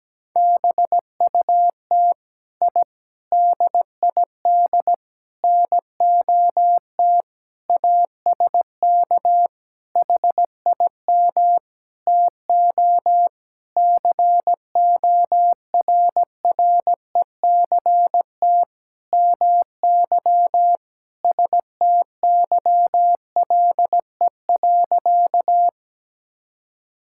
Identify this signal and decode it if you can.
Morse code, heard as a keyed tone, about 17 wpm, 700 Hz. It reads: BUT I DID NOT ASK HIM TO CORRECT MY STYLE.
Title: Sentences from Top 1000 Words - Review Part 5 - 17wpm